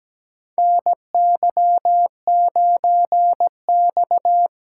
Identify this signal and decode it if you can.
Morse code, heard as a keyed tone, about 17 wpm, 700 Hz. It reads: NY9X
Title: K1USN SST QSO Practice - 17wpm